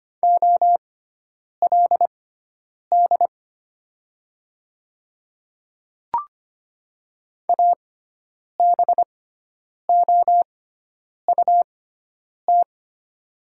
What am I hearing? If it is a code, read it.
Morse code, heard as a keyed tone, about 25 wpm, spaced as for 9 wpm, 700 Hz. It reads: OLD ABOUT
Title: 6x Character Spacing - Top 200 Words - 25wpm